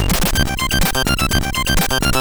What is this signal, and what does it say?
There is no code.